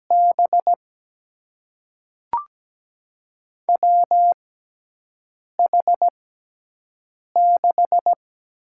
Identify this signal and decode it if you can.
Morse code, heard as a keyed tone, about 17 wpm, spaced as for 4 wpm, 700 Hz. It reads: BWH 6
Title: USA Call Signs - 5 Characters - 6x Character Spacing - 17wpm